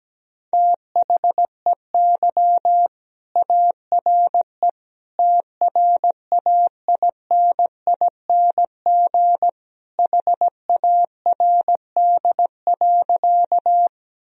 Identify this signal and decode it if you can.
Morse code, heard as a keyed tone, about 17 wpm, 700 Hz. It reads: THEY ARE TRAINING HARD.
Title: Sentences from Top 1000 Words - Review Part 4 - 17wpm